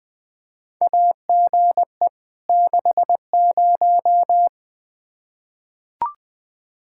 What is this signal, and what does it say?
Morse code, heard as a keyed tone, about 20 wpm, 700 Hz. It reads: AGE 60
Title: Practice QSO - Exchange 3 - More Info - 20wpm